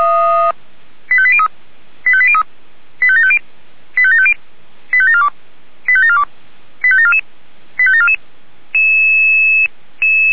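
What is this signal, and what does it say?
Five-tone selective call (ZVEI), heard as quick runs of five tones, tones of about 70 ms. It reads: 85803 85803 85750 85750 85742 85742 85744 85744
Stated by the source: ZVEI five-tone call received via SDR.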